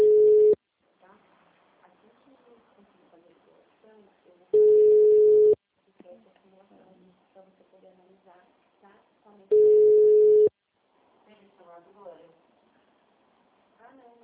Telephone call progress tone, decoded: ringback tone

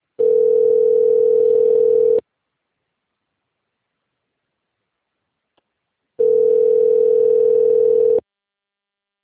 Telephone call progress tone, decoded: ringback tone

